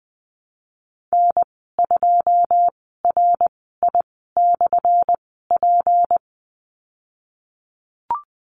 Morse code, heard as a keyed tone, about 20 wpm, spaced as for 15 wpm, 700 Hz. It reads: N2RI/P
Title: 179 Lesson 29g - Slash - Call Signs 20wpm